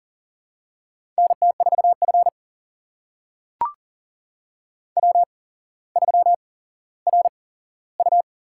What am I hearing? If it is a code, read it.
Morse code, heard as a keyed tone, about 40 wpm, spaced as for 17 wpm, 700 Hz. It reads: NT4F W3RU
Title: USA Call Signs - 4 Characters - 8x Character Spacing - 40wpm